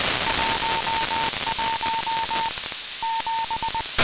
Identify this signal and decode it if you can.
Morse code, heard as a keyed tone, about 20 wpm, 910 Hz, with noise. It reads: 11 7
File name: humsatd 1100utc